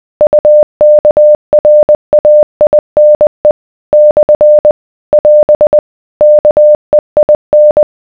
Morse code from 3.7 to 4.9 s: /